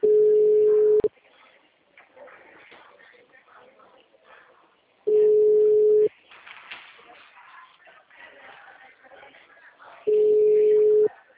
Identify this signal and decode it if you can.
Ringback tone (Europe, 425 Hz), with speech.
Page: Limeira - SP